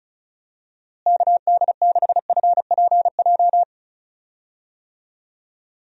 Morse code, heard as a keyed tone, about 34 wpm, 700 Hz. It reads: KD6FPJ
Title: USA Call Signs - 6 Characters - Repeated 3x - 35wpm